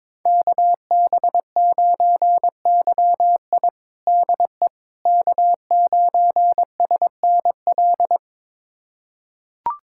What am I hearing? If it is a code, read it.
Morse code, heard as a keyed tone, about 22 wpm, 700 Hz. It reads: KB9YI DE K9SNL
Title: USA Callsign DE Callsign - 5 Characters - 22wpm